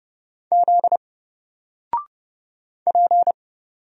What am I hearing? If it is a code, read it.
Morse code, heard as a keyed tone, about 31 wpm, 700 Hz. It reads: Z P
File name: Single-Letter-Number-30wpm